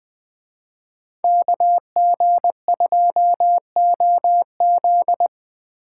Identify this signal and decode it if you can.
Morse code, heard as a keyed tone, about 20 wpm, 700 Hz. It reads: KG2OZ
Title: Speed Racing - USA 2x2 Call Signs 17wpm